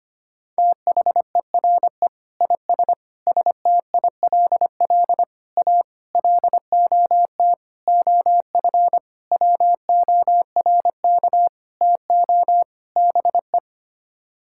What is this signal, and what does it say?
Morse code, heard as a keyed tone, about 25 wpm, 700 Hz. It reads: THERE IS STILL A LOT OF WORK TO BE
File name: Sets of 9 Words - 25wpm